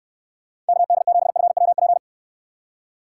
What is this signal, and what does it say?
Morse code, heard as a keyed tone, about 69 wpm, 700 Hz. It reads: KD7FWP